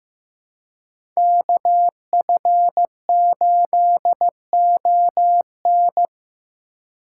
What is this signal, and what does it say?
Morse code, heard as a keyed tone, about 18 wpm, 700 Hz. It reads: KF8ON
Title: Speed Racing - USA 2x2 Call Signs 15wpm